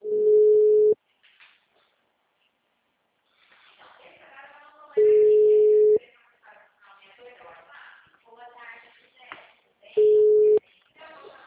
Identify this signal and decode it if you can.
Ringback tone (Europe, 425 Hz), with speech.